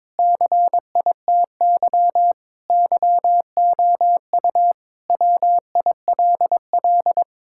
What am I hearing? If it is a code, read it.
Morse code, heard as a keyed tone, about 22 wpm, 700 Hz. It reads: CITY YOU WILL
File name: practice / 22wpm / Sentences with 3-Word Phrases - 22wpm